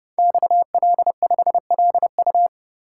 Morse code, heard as a keyed tone, about 30 wpm, 700 Hz. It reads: XL5LU
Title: Code Groups - 5 Letter:Numbers - 30wpm